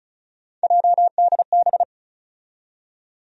Morse code, heard as a keyed tone, about 36 wpm, 700 Hz. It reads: JDB
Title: USA Call Sign Postfixes - 3 Letters 35wpm